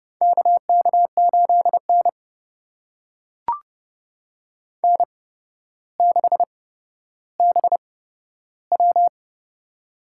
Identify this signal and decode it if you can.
Morse code, heard as a keyed tone, about 30 wpm, spaced as for 13 wpm, 700 Hz. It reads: KK8N N6BW
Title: USA Call Signs - 4 Characters - 8x Character Spacing - 30wpm